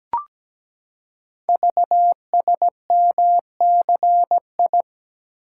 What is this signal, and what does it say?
Morse code, heard as a keyed tone, about 17 wpm, 700 Hz. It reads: VSMCI
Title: Code Groups - 5 Letter:Numbers - 17wpm